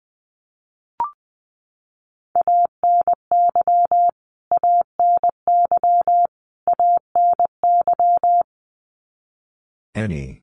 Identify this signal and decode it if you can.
Morse code, heard as a keyed tone, about 20 wpm, 700 Hz. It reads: ANY ANY ANY
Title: Top 300 Words Repeated 20wpm